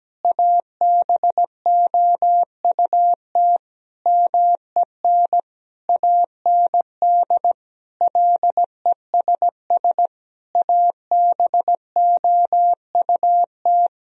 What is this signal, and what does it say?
Morse code, heard as a keyed tone, about 17 wpm, 700 Hz. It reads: ABOUT MEN AND LESS ABOUT
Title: Sentences from Top 1000 Words - Review Part 2 - 17wpm